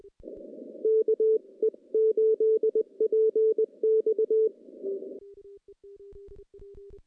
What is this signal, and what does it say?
Morse code, heard as a keyed tone, about 20 wpm, 440 Hz. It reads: KE8PX
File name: contests / CW-SS-k5tr- -0100